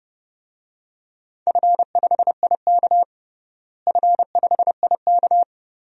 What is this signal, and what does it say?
Morse code, heard as a keyed tone, about 32 wpm, 700 Hz. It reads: F5IK F5IK
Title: DXCC-100 Call Signs - 4 Characters - Repeated 2x - 30wpm